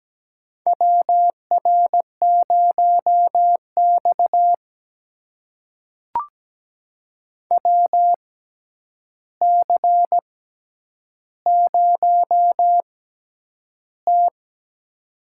Morse code, heard as a keyed tone, about 17 wpm, spaced as for 9 wpm, 700 Hz. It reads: WR0X WC0T
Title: USA Call Signs - 4 Characters - 6x Character Spacing - 17wpm